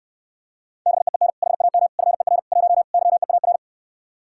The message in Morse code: BEEN HAM FER 1 YEAR

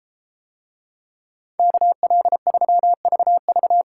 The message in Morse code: KL3VV

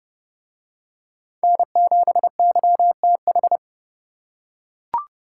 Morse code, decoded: N7YTH

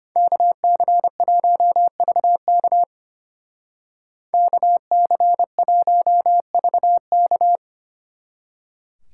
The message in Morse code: KC1VK KC1VK

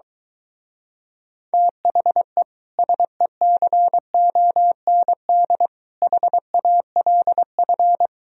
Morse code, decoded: THE SECOND HALF

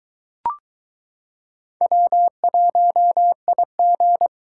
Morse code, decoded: W1IG

